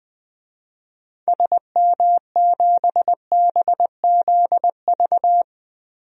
Morse code, heard as a keyed tone, about 20 wpm, 700 Hz. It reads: SM7BZV
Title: DXCC-100 Call Signs - 6 Characters - Repeated 3x - 20wpm